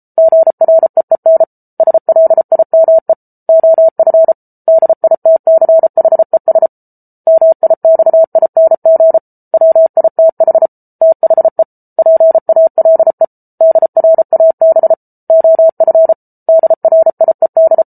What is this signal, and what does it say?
Morse code, heard as a keyed tone, about 34 wpm, 650 Hz. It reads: GREEN SLIME OF DITCHES MIXING WITH THE PALE DRAB OF DRIED